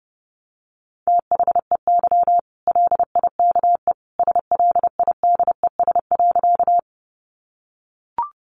Morse code, heard as a keyed tone, about 30 wpm, 700 Hz. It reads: THEY LIKE SLIDES.
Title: Easy Phrases 30wpm